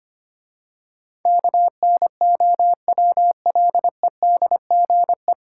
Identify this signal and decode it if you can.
Morse code, heard as a keyed tone, about 25 wpm, 700 Hz. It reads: KNOWLEDGE
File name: practice / 25wpm / 9 Letter Words - First 3 Letters Given - 25wpm